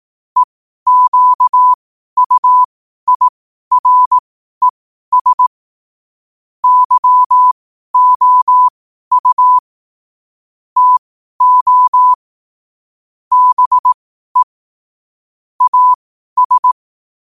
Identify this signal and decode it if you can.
Morse code, heard as a keyed tone, about 18 wpm, spaced as for 12 wpm, 1000 Hz. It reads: EQUIRES YOU TO BE AS
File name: qotdfiles / 12wpm